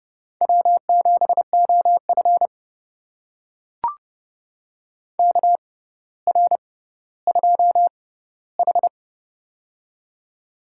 Morse code, heard as a keyed tone, about 30 wpm, spaced as for 15 wpm, 700 Hz. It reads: W7OF KR2H
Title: USA Call Signs - 4 Characters - 6x Character Spacing - 30wpm